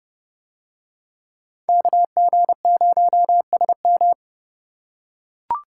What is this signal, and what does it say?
Morse code, heard as a keyed tone, about 30 wpm, 700 Hz. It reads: KG0SM